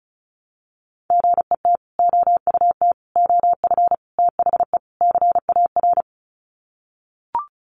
Morse code, heard as a keyed tone, about 34 wpm, 700 Hz. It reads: GET OUT OF THE CAR
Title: Sets of 5 Words Encore2 35wpm